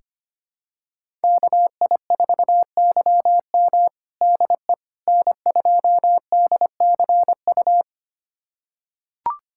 Morse code, heard as a keyed tone, about 25 wpm, 700 Hz. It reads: KI4YM DE N2DCU
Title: USA Callsign DE Callsign - 5 Characters - 25wpm